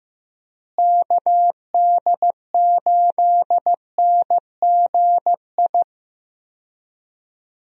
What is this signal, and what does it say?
Morse code, heard as a keyed tone, about 15 wpm, 700 Hz. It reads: KD8NGI
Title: USA Call Signs - 6 Characters - Repeated 2x - 15wpm